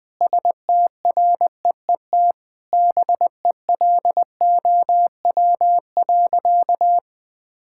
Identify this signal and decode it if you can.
Morse code, heard as a keyed tone, about 21 wpm, 700 Hz. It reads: STREET BELOW.
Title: Sentences from Top 1000 Words - Review Part 5 - 20wpm